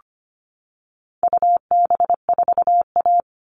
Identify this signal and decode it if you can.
Morse code, heard as a keyed tone, about 25 wpm, 700 Hz. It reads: UB4A
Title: All Cut Numbers - 4-Digit 25wpm